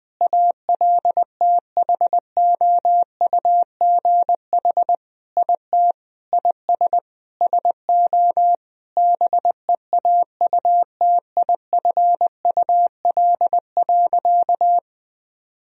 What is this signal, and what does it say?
Morse code, heard as a keyed tone, about 20 wpm, 700 Hz. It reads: ALTHOUGH IT IS SO BEAUTIFUL.
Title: Sentences from Top 1000 Words - Review Part 3 - 20wpm